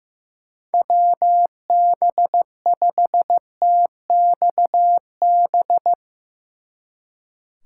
Morse code, heard as a keyed tone, about 16 wpm, 700 Hz. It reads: WB5TXB
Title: Speed Racing - USA 2x3 Call Signs 15wpm